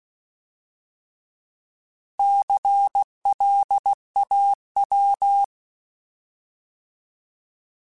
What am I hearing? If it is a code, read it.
Morse code, heard as a keyed tone, about 16 wpm, 800 Hz. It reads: CLAW
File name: Alphabetical / Alphabetical mp3s / Cs @15